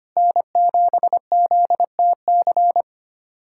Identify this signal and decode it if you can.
Morse code, heard as a keyed tone, about 25 wpm, 700 Hz. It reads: N7ZTC
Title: USA Callsign DE Callsign - 5 Characters - 25wpm